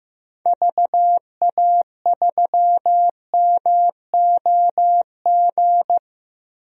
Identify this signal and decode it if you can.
Morse code, heard as a keyed tone, about 15 wpm, 700 Hz. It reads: VA3MOG
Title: DXCC-100 Call Signs - 6 Characters - Repeated 2x - 15wpm